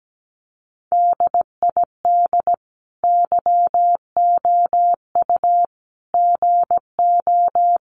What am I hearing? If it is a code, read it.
Morse code, heard as a keyed tone, about 17 wpm, 700 Hz. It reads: DID YOU GO
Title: Sentences from Top 1200 Words 17wpm